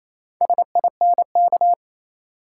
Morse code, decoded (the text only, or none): SINK